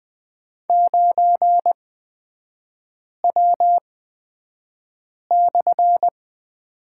Morse code, heard as a keyed tone, about 20 wpm, spaced as for 5 wpm, 700 Hz. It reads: 9W/
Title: 224 Lesson 36e - BT - Three Characters 20wpm